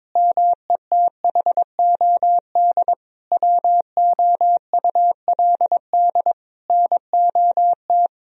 Morse code, heard as a keyed tone, about 22 wpm, 700 Hz. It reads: METHOD WOULD NOT
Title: Sentences from Top 1000 Words - Review Part 3 - 22wpm